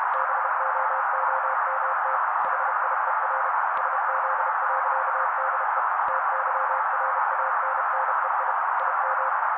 Morse code, heard as a keyed tone, about 32 wpm, 530 Hz. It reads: BOOMT HIER SZ.DE TYRANNEI IM
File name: daily / 31wpm noise